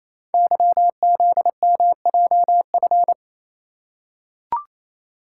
Morse code, heard as a keyed tone, about 29 wpm, 700 Hz. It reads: YZMJF
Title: Code Groups - 5 Letters - 28wpm